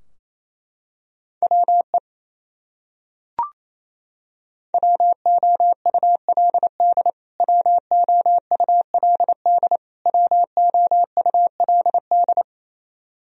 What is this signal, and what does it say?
Morse code, heard as a keyed tone, about 28 wpm, 700 Hz. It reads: WE WOULD WOULD WOULD
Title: Top 100 Words Repeated 28wpm